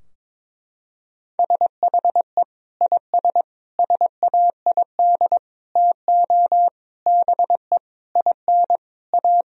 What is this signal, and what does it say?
Morse code, heard as a keyed tone, about 22 wpm, 700 Hz. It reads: SHE IS SAID TO BE IN A
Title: Sentences from Top 1000 Words - Review Part 2 - 22wpm